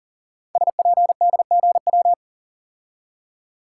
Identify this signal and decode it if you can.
Morse code, heard as a keyed tone, about 43 wpm, 690 Hz. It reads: SPDGW
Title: Code Groups - 5 Letters - 40wpm